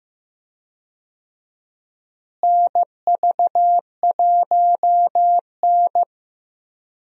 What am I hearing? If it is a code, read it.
Morse code, heard as a keyed tone, about 15 wpm, 700 Hz. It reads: NV1N
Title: USA Call Signs - 4 Characters - Repeated 3x - 15wpm